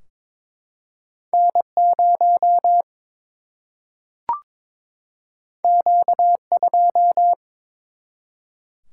Morse code, heard as a keyed tone, about 22 wpm, 700 Hz. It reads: N 0 Q 2
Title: Single-Letter Single-Number 22wpm